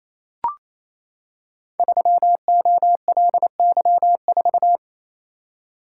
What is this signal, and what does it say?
Morse code, heard as a keyed tone, about 27 wpm, 700 Hz. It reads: 3OLY4
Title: Code Groups - 5 Letter:Numbers - 28wpm